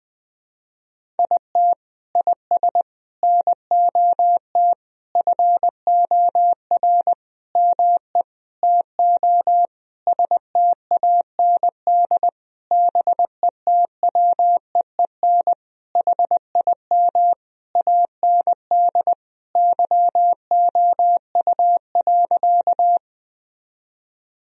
Morse code, heard as a keyed tone, about 20 wpm, 700 Hz. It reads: IT IS NOT FOR ME TO STAND BETWEEN HIM AND YOU.
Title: Sentences from Top 1000 Words - Review Part 1 - 20wpm